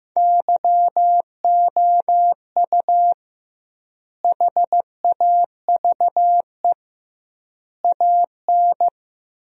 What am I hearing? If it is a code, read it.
Morse code, heard as a keyed tone, about 15 wpm, 700 Hz. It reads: YOU HAVE AN
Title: 2x Word-Spacing - Sentences from Top 400 Words - 15wpm